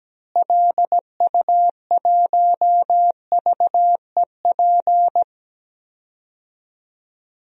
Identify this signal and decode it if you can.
Morse code, heard as a keyed tone, about 17 wpm, 700 Hz. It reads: LU1VEP
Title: DXCC-100 Call Signs - 6 Characters - Repeated 2x - 17wpm